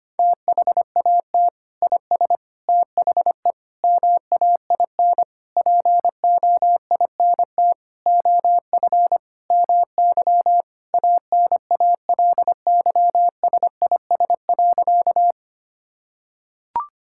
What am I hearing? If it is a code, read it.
Morse code, heard as a keyed tone, about 25 wpm, 700 Hz. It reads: THAT IS THE MAIN POINT OF MY ANALYSIS.
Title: Sentences from Top 1000 Words - Review Part 5 - 25wpm